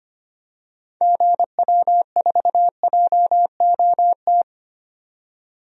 Morse code, heard as a keyed tone, about 25 wpm, 700 Hz. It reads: GW4JOT
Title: DXCC-100 Call Signs - 6 Characters - 25wpm